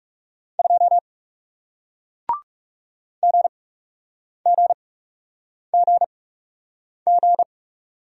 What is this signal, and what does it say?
Morse code, heard as a keyed tone, about 40 wpm, spaced as for 8 wpm, 700 Hz. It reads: 2 GGGG